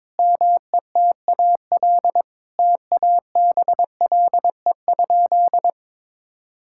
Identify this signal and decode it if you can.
Morse code, heard as a keyed tone, about 22 wpm, 700 Hz. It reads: METAL TABLE?
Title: Sentences from Top 1000 Words - Review Part 6 - 22wpm